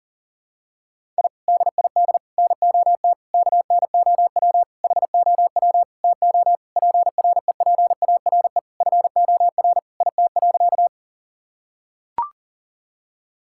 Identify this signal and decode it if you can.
Morse code, heard as a keyed tone, about 37 wpm, 700 Hz. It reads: I DID NOT KNOW HOW TO PREPARE FOR IT.
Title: Sentences from Top 1000 Words - Review Part 6 - 40wpm